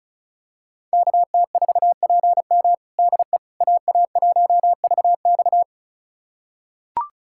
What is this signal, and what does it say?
Morse code, heard as a keyed tone, about 34 wpm, 700 Hz. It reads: KT4PM DE AA1VX
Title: USA Callsign DE Callsign - 5 Characters - 35wpm